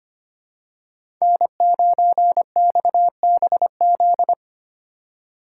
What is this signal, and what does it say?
Morse code, heard as a keyed tone, about 25 wpm, 700 Hz. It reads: N9XBZ
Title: USA Call Signs - 5 Characters - 3x Character Spacing - 25wpm